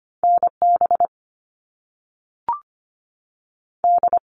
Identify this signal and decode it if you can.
Morse code, heard as a keyed tone, about 25 wpm, 700 Hz. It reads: NB D
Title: All Cut Numbers - 2-Digit 25wpm